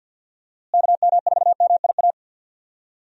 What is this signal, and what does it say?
Morse code, heard as a keyed tone, about 49 wpm, 700 Hz. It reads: KM4MIA
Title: USA Call Signs - 6 Characters - Repeated 3x - 50wpm